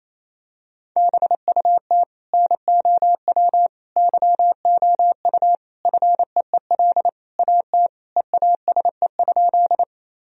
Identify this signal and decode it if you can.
Morse code, heard as a keyed tone, about 28 wpm, 700 Hz. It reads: BUT NOW YOU FEEL AT EASE?
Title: Sentences from Top 1000 Words - Review Part 1 - 28wpm